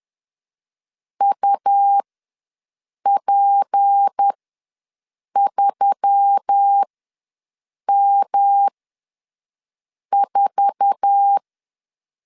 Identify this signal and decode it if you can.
Morse code, heard as a keyed tone, about 10 wpm, spaced as for 6 wpm, 800 Hz. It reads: UP3M4
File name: LEZIONI / 8 YJ45 / 8 YJ4 ws 10wpm 4020